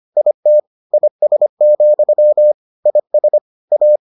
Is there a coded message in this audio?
Yes: Morse code from 0.2 to 4.0 s, heard as a keyed tone, about 25 wpm, 600 Hz. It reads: IT IS, IS A